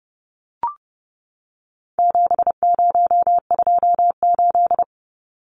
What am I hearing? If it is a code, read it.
Morse code, heard as a keyed tone, about 30 wpm, 700 Hz. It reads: 7028
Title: Numbers - 4-Digit 30wpm